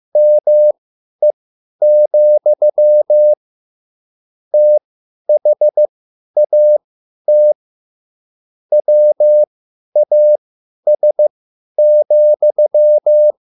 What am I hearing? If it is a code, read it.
Morse code, heard as a keyed tone, about 15 wpm, spaced as for 10 wpm, 600 Hz. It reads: ME, THAT WAS,
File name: QSL / CW lesson